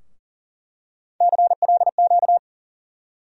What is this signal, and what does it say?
Morse code, heard as a keyed tone, about 40 wpm, 700 Hz. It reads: CLQ
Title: USA Call Sign Postfixes - 3 Letters 40wpm